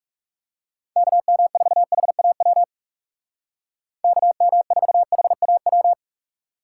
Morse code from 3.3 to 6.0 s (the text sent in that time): KM4HAW